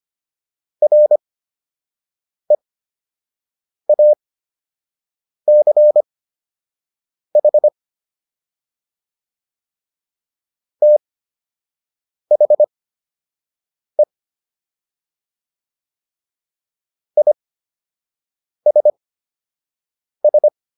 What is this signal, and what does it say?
Morse code, heard as a keyed tone, about 25 wpm, spaced as for 6 wpm, 600 Hz. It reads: REACH THE ISS